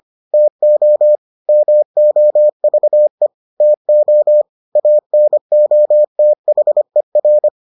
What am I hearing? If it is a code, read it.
Morse code, heard as a keyed tone, about 25 wpm, 600 Hz. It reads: TO MOVE TO ANOTHER